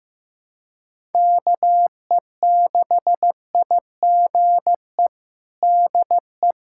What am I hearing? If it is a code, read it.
Morse code, heard as a keyed tone, about 15 wpm, 700 Hz. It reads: KE6IGE DE
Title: USA Callsign DE Callsign - 6 Characters - 15wpm